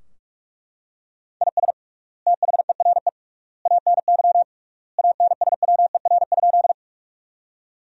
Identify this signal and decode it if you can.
Morse code, heard as a keyed tone, about 41 wpm, 700 Hz. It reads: IS THERE ANY ANSWER?
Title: 3x Word-Spacing - Sentences from Top 300 Words - 45wpm